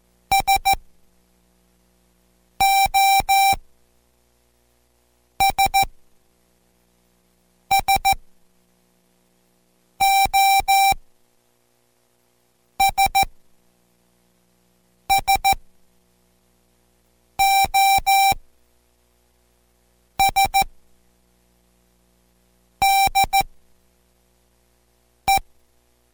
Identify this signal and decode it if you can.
Morse code, heard as a keyed tone, about 14 wpm, spaced as for 4 wpm, 780 Hz. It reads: SOS SOS SOS DE